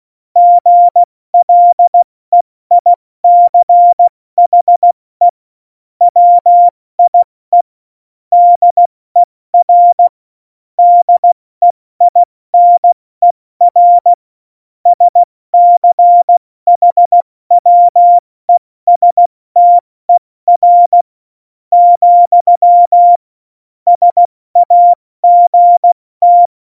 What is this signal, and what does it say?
Morse code, heard as a keyed tone, about 16 wpm, spaced as for 14 wpm, 700 Hz. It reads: GLEICHE WIE DER DEINER SCHWESTER, SAGT